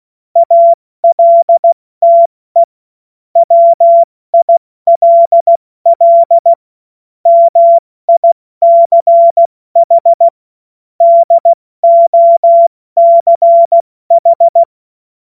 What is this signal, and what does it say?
Morse code, heard as a keyed tone, about 16 wpm, 680 Hz. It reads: ALTE WILL MICH DOCH
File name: Morsekurs / Witze / Witz 01. November 2022 70 F80